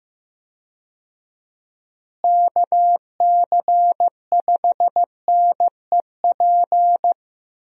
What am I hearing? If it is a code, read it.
Morse code, heard as a keyed tone, about 15 wpm, 700 Hz. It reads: KC5NEP
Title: USA Call Signs - 6 Characters - Repeated 2x - 15wpm